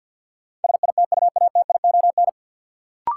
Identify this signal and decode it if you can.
Morse code, heard as a keyed tone, about 49 wpm, 700 Hz. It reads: SITUATION
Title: Sentences from Top 1200 Words - New Word Drill Alt - 50wpm